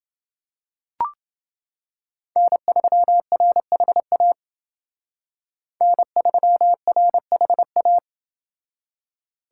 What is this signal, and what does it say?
Morse code, heard as a keyed tone, about 28 wpm, 700 Hz. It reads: N3RHA N3RHA